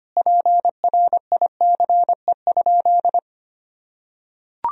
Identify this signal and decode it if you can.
Morse code, heard as a keyed tone, about 25 wpm, 700 Hz. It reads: PRICE?
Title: Sentences from Top 1000 Words - Review Part 3 - 25wpm